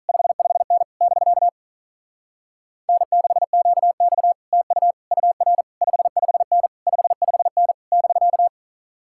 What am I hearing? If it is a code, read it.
Morse code, heard as a keyed tone, about 44 wpm, 700 Hz. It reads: 55N <BK> N6QX TU UR 55N 55N <BK>